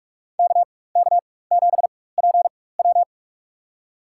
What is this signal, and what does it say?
Morse code, heard as a keyed tone, about 43 wpm, spaced as for 21 wpm, 700 Hz. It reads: KK7PW
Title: USA Call Signs - 5 Characters - 4x Character Spacing - 45wpm